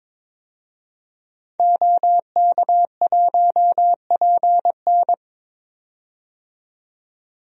Morse code, heard as a keyed tone, about 22 wpm, 700 Hz. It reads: OK1PN
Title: DXCC-100 Call Signs - 5 Characters - 22wpm